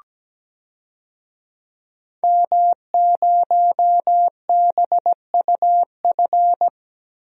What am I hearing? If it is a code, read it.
Morse code, heard as a keyed tone, about 17 wpm, 700 Hz. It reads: M0BUF
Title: DXCC-100 Call Signs - 5 Characters - 17wpm